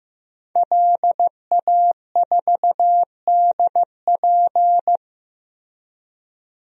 Morse code, heard as a keyed tone, about 15 wpm, 700 Hz. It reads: LA4DP
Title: DXCC-100 Call Signs - 5 Characters - Repeated 2x - 15wpm